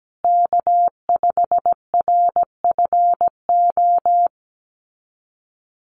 Morse code, heard as a keyed tone, about 17 wpm, 700 Hz. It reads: K5RFO
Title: USA Call Signs Repeated 2x 17wpm